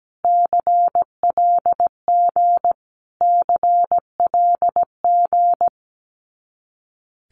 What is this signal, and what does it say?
Morse code, heard as a keyed tone, about 17 wpm, 700 Hz. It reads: CLG CLG
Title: Abbreviations Repeated 3x - 17wpm